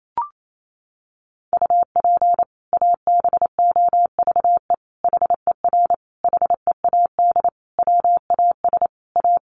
Morse code, heard as a keyed tone, about 28 wpm, 700 Hz. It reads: UP ABOVE HER HEAD WAS A
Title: Sentences from Top 1200 Words 28wpm